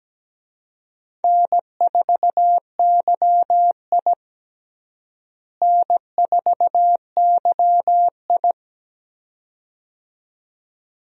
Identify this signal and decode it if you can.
Morse code, heard as a keyed tone, about 17 wpm, 700 Hz. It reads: N4YI N4YI
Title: USA Call Signs - 4 Characters - Repeated 2x - 17wpm